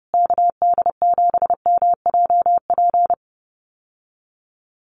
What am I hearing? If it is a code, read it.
Morse code, heard as a keyed tone, about 30 wpm, 700 Hz. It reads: KD7MJP
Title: USA Call Signs Repeated 3x 30wpm